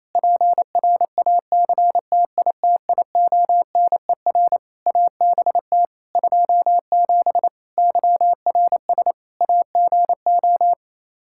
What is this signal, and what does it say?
Morse code, heard as a keyed tone, about 27 wpm, 700 Hz. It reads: PRACTITIONER ABT 27 YRS AGO
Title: Practice QSO - Exchange 3 - More Info - 28wpm